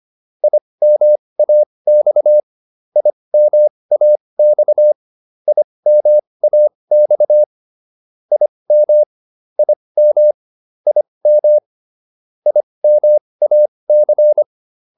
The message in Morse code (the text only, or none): IMAX IMAX IMAX IM IM IM IMAC